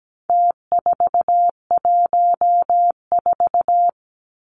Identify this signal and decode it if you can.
Morse code, heard as a keyed tone, about 17 wpm, 700 Hz. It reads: T414
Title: Numbers - Contest Style - 4-Digit 17wpm